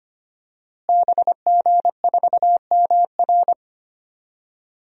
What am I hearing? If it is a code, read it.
Morse code, heard as a keyed tone, about 25 wpm, 700 Hz. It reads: BG4MR
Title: DXCC-100 Call Signs - 5 Characters - Repeated 2x - 25wpm